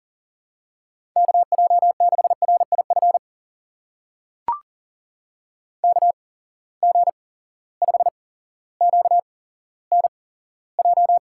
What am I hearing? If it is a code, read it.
Morse code, heard as a keyed tone, about 37 wpm, spaced as for 17 wpm, 700 Hz. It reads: KJ6RIF KG5QNJ
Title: USA Call Signs - 8x Character Spacing - 40wpm